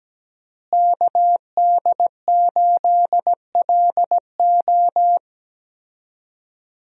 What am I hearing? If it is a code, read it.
Morse code, heard as a keyed tone, about 17 wpm, 700 Hz. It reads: KD8LO
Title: USA Call Signs - 5 Characters - Repeated 3x - 17wpm